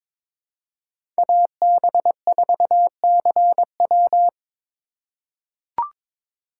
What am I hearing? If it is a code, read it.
Morse code, heard as a keyed tone, about 22 wpm, 700 Hz. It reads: AB4CW
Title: USA Call Signs - 2x Character Spacing - 22wpm